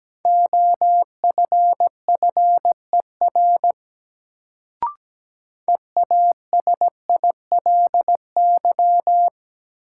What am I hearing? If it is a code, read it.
Morse code, heard as a keyed tone, about 17 wpm, 700 Hz. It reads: OFFER EASILY
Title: Sentences from Top 900 Words - New Word Drill - 17wpm